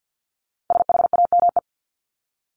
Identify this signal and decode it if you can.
Morse code, heard as a keyed tone, about 48 wpm, 700 Hz. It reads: SHAME
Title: Top 1751-2000 Words 50wpm